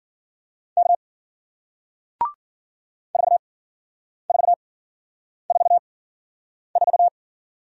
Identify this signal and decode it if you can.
Morse code, heard as a keyed tone, about 53 wpm, spaced as for 9 wpm, 700 Hz. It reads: K 4444